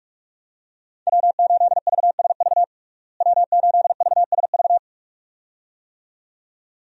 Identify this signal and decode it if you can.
Morse code, heard as a keyed tone, about 42 wpm, 700 Hz. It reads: W8VSV W8VSV
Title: USA Call Signs - 5 Characters - Repeated 2x - 45wpm